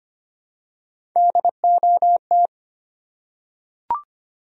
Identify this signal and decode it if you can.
Morse code, heard as a keyed tone, about 25 wpm, 700 Hz. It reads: DOT
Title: USA Common Names - Max 3 Letters - 25wpm